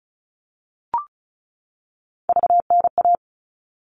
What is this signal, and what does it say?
Morse code, heard as a keyed tone, about 34 wpm, 700 Hz. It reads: VNA
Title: All Cut Numbers - 3-Digit 35wpm